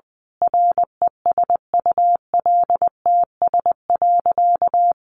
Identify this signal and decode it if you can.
Morse code, heard as a keyed tone, about 20 wpm, 700 Hz. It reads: RESULTS.
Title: A Sound Of Thunder 20wpm